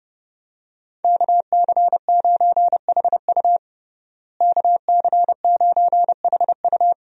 Morse code, heard as a keyed tone, about 30 wpm, 700 Hz. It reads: KC9HU KC9HU
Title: USA Call Signs - 5 Characters - Repeated 3x - 30wpm